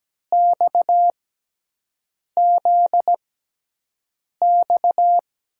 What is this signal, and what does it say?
Morse code, heard as a keyed tone, about 17 wpm, spaced as for 6 wpm, 700 Hz. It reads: XZX
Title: USA Call Signs - 6x Character Spacing - 17wpm